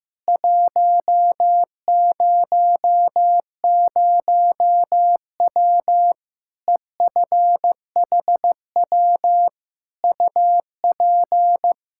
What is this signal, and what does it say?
Morse code, heard as a keyed tone, about 15 wpm, 700 Hz. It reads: 100W EFHW UP
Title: K1USN SST QSO Practice 15wpm